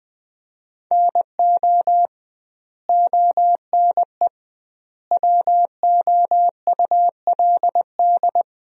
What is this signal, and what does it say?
Morse code, heard as a keyed tone, about 20 wpm, 700 Hz. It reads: NO ONE WOULD
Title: 2x Word-Spacing - Sentences from Top 300 Words - 20wpm